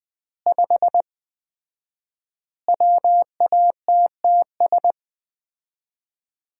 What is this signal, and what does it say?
Morse code, heard as a keyed tone, about 21 wpm, 700 Hz. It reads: 5 WATTS
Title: Common QSO Phrases - 4x Word-Spacing - 20wpm